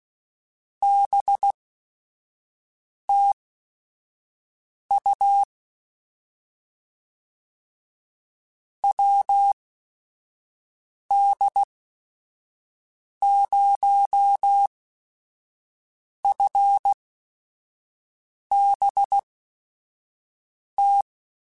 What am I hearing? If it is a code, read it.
Morse code, heard as a keyed tone, about 16 wpm, spaced as for 5 wpm, 800 Hz. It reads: BTU WD0FBT
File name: QSOs / QSOs-mp3s / WD0FBT 1@5